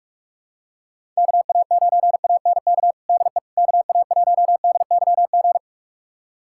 Morse code, heard as a keyed tone, about 42 wpm, 700 Hz. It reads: KA9ANK DE KA1DYG
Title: USA Callsign DE Callsign - 6 Characters - 45wpm